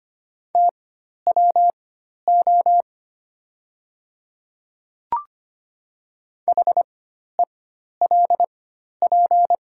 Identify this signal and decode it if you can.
Morse code, heard as a keyed tone, about 25 wpm, spaced as for 12 wpm, 700 Hz. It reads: TWO HELP
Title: 4x Character Spacing - Top 200 Words - 25wpm